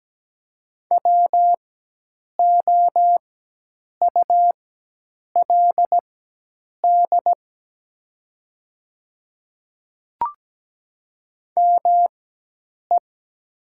Morse code, heard as a keyed tone, about 17 wpm, spaced as for 8 wpm, 700 Hz. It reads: WOULD ME
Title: 4x Character Spacing - Top 300 Words - 17wpm